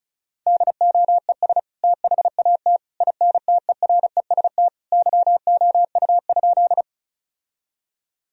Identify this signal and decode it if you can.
Morse code, heard as a keyed tone, about 36 wpm, 700 Hz. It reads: DOES THAT INTEREST YOU?
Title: Sentences from Top 1000 Words - Review Part 2 - 35wpm